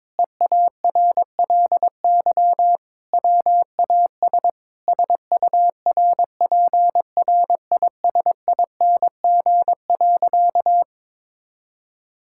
Morse code, heard as a keyed tone, about 22 wpm, 700 Hz. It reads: EARLY WAS SURPRISING.
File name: practice / 22wpm / Sentences with 4-Word Phrases - 22wpm